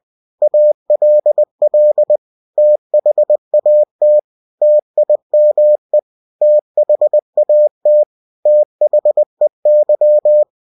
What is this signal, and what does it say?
Morse code, heard as a keyed tone, about 20 wpm, 600 Hz. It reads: ALL THAT TIME THAT THEY